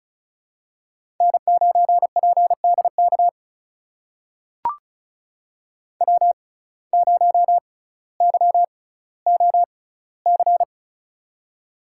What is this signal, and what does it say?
Morse code, heard as a keyed tone, about 36 wpm, spaced as for 18 wpm, 700 Hz. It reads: N9PDK W0YOC
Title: USA Call Signs - 6x Character Spacing - 35wpm